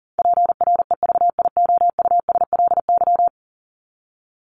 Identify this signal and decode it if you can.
Morse code, heard as a keyed tone, about 40 wpm, 700 Hz. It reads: PREVIOUSLY
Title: LY Suffix - Top 100 Words with Base Words 40wpm